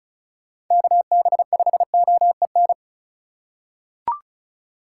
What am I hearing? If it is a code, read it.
Morse code, heard as a keyed tone, about 36 wpm, 690 Hz. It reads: KB5OEN
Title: USA Call Signs - 6 Characters - 4x Character Spacing - 35wpm